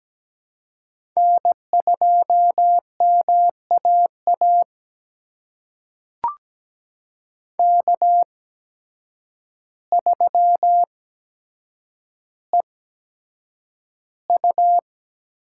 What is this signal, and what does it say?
Morse code, heard as a keyed tone, about 17 wpm, spaced as for 8 wpm, 700 Hz. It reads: N2MAA K3EU
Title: USA Call Signs - 5 Characters - 8x Character Spacing - 17wpm